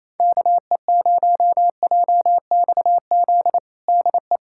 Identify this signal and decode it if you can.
Morse code, heard as a keyed tone, about 28 wpm, 700 Hz. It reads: KE0JXZ DE
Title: USA Callsign DE Callsign 28wpm